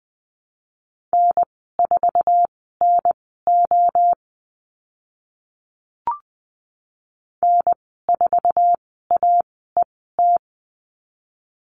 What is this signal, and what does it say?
Morse code, heard as a keyed tone, about 20 wpm, spaced as for 14 wpm, 700 Hz. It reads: N4NO N4AET
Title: 055 Lesson 10g - Letter r - Call Signs 20wpm